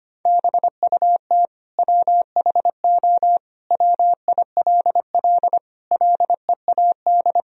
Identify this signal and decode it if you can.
Morse code, heard as a keyed tone, about 25 wpm, 700 Hz. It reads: BUT WHO WILL LEAD
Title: Sentences from Top 1000 Words - Review Part 2 - 25wpm